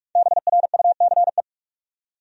Morse code, heard as a keyed tone, about 44 wpm, 700 Hz. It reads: BRUCE
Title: USA Common Names - 45wpm